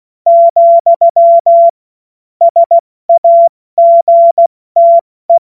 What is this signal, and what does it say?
Morse code, heard as a keyed tone, about 16 wpm, 680 Hz. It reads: , SAGTE
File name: Morsekurs / Witze / Witz 07. März 2023 70 F80